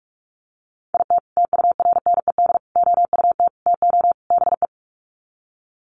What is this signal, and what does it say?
Morse code, heard as a keyed tone, about 44 wpm, 700 Hz. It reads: IT TURNED OUT TO BE
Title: Sets of 5 Words 45wpm